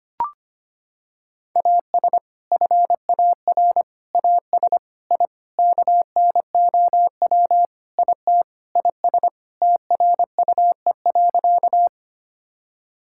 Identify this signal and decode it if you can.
Morse code, heard as a keyed tone, about 25 wpm, 700 Hz. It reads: AS FAR AS I KNOW IT IS TRUE.
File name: practice / 25wpm / Sentences with 4-Word Phrases - 25wpm